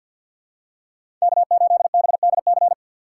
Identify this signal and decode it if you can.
Morse code, heard as a keyed tone, about 48 wpm, 700 Hz. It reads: K8BDC
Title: USA Call Signs - 5 Characters - 6x Character Spacing - 50wpm